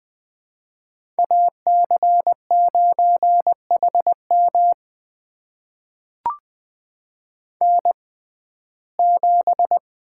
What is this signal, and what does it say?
Morse code, heard as a keyed tone, about 20 wpm, spaced as for 15 wpm, 700 Hz. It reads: AC9HM N7
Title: USA Call Signs - 5 Characters - 6x Character Spacing - 20wpm